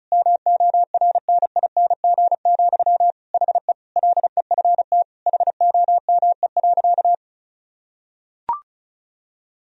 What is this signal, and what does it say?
Morse code, heard as a keyed tone, about 36 wpm, 700 Hz. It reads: MORNING, HE LEFT HOME.
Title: Sentences from Top 1000 Words - Review Part 4 - 35wpm